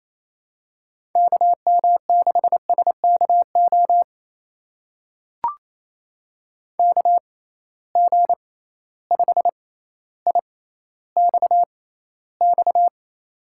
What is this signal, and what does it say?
Morse code, heard as a keyed tone, about 28 wpm, spaced as for 14 wpm, 700 Hz. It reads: KM6SKO KG5IXX
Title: USA Call Signs - 6 Characters - 6x Character Spacing - 28wpm